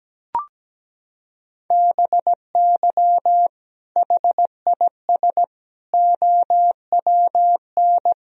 Morse code, heard as a keyed tone, about 17 wpm, 700 Hz. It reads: BY HIS OWN
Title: Sentences from Top 1000 Words - Review Part 1 - 17wpm